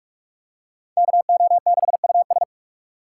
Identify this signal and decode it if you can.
Morse code, heard as a keyed tone, about 42 wpm, 700 Hz. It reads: KO6US
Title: USA Call Signs - 4x Character Spacing - 45wpm